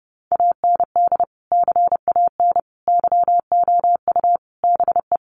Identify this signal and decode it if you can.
Morse code, heard as a keyed tone, about 30 wpm, 700 Hz. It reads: AND CAN YOU BE
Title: Sentences from Top 1100 Words 30wpm